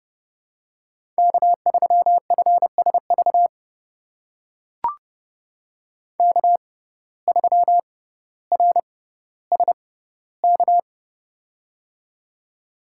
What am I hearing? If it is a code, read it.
Morse code, heard as a keyed tone, about 30 wpm, spaced as for 15 wpm, 700 Hz. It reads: K3FSV K3RSK
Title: USA Call Signs - 5 Characters - 6x Character Spacing - 30wpm